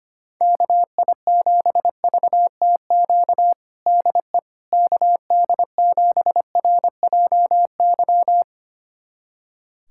Morse code, heard as a keyed tone, about 25 wpm, 700 Hz. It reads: KI7VTQ DE KD7RJY
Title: USA Callsign DE Callsign - 6 Characters - 25wpm